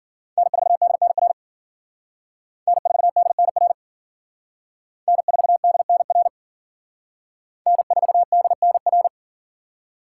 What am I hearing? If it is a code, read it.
Morse code, heard as a keyed tone, about 53 wpm, 700 Hz. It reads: N4DNR N4DNR N4DNR N4DNR